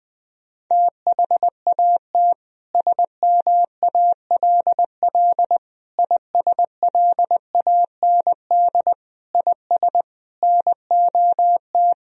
Morse code, heard as a keyed tone, about 21 wpm, 700 Hz. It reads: THAT SMALL ISLAND IS NOT
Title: Sentences from Top 1000 Words - Review Part 4 - 20wpm